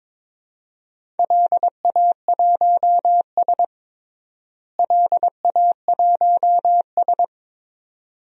Morse code, heard as a keyed tone, about 22 wpm, 700 Hz. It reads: LA1S LA1S
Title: DXCC-100 Call Signs - 4 Characters - Repeated 2x - 22wpm